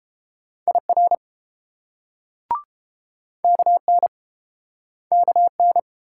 Morse code, heard as a keyed tone, about 31 wpm, 700 Hz. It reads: IR KN KN